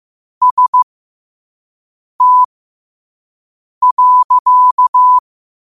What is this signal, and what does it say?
Morse code, heard as a keyed tone, about 15 wpm, spaced as for 5 wpm, 1000 Hz. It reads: ST.
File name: qotdfiles / 5wpm